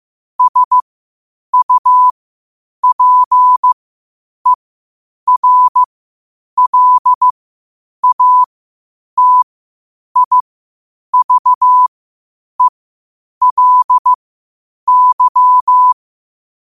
Morse code, heard as a keyed tone, about 15 wpm, spaced as for 8 wpm, 1000 Hz. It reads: SUPERLATIVELY
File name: qotdfiles / 8wpm